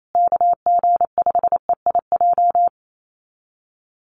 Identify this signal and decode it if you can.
Morse code, heard as a keyed tone, about 28 wpm, 700 Hz. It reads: KG5EIJ
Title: USA Call Signs Repeated 3x 28wpm